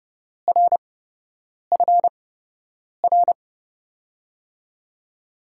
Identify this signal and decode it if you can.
Morse code, heard as a keyed tone, about 32 wpm, spaced as for 8 wpm, 700 Hz. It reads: RFR
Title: USA Call Signs - 8x Character Spacing - 30wpm